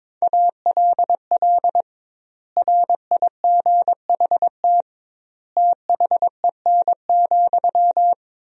Morse code, heard as a keyed tone, about 22 wpm, 700 Hz. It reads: ALL RIGHT THEN,
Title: 2x Word-Spacing - Sentences from Top 500 Words - 22wpm